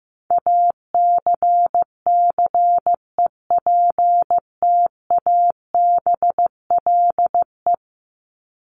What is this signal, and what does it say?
Morse code, heard as a keyed tone, about 15 wpm, 700 Hz. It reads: ACCEPTABLE
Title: Top 1501-1750 Words 15wpm